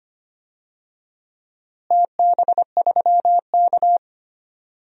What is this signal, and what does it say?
Morse code, heard as a keyed tone, about 25 wpm, 700 Hz. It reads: TB3K
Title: DXCC-100 Call Signs - 4 Characters - Repeated 2x - 25wpm